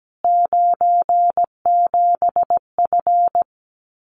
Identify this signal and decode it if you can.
Morse code, heard as a keyed tone, about 17 wpm, 700 Hz. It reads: 97F
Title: Common QSO Phrases Encore 17wpm